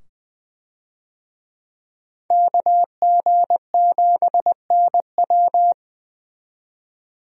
Morse code, heard as a keyed tone, about 20 wpm, 700 Hz. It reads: KG7NW
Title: USA Call Signs - 5 Characters - 20wpm